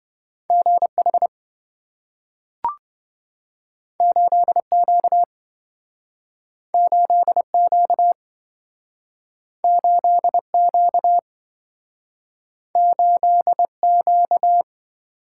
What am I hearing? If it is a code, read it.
Morse code, heard as a keyed tone, about 26 wpm, 700 Hz. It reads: GH 8Q 8Q 8Q 8Q